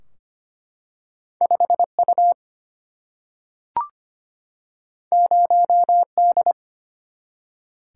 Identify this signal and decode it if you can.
Morse code, heard as a keyed tone, about 25 wpm, 700 Hz. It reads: 5U 0D
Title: Single-Letter Single-Number 25wpm